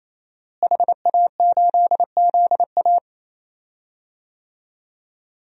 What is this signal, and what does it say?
Morse code, heard as a keyed tone, about 28 wpm, 700 Hz. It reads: HA8ZA
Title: DXCC-100 Call Signs - 5 Characters - Repeated 3x - 28wpm